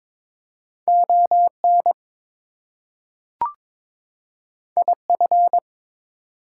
Morse code, heard as a keyed tone, about 22 wpm, 700 Hz. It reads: ON IF
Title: Two Letter Words - 22wpm